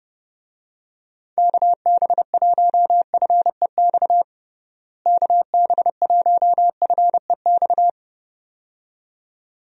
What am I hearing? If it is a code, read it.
Morse code, heard as a keyed tone, about 30 wpm, 700 Hz. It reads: KB1FEX KB1FEX
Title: USA Call Signs - 6 Characters - Repeated 2x - 30wpm